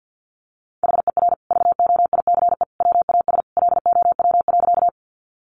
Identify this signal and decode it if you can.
Morse code, heard as a keyed tone, about 50 wpm, 700 Hz. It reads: HER VOICE WAS LOW.
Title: Sentences from Top 500 Words 50wpm